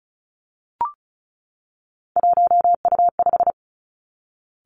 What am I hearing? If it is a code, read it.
Morse code, heard as a keyed tone, about 36 wpm, 700 Hz. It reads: 1U5
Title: Three Challenging Letter-Number Combinations 35wpm